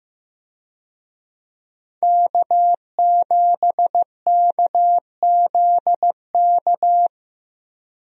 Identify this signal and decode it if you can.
Morse code, heard as a keyed tone, about 15 wpm, 700 Hz. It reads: K7KZK
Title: USA Call Signs - 5 Characters - Repeated 2x - 15wpm